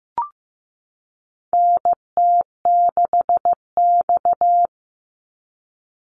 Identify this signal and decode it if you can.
Morse code, heard as a keyed tone, about 15 wpm, 700 Hz. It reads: NT6X
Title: Call Signs Repeated 3x - 15wpm